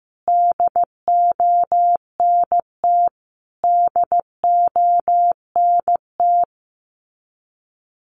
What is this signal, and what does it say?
Morse code, heard as a keyed tone, about 15 wpm, 700 Hz. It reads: DONT DONT
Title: Top 500 Words Repeated 15wpm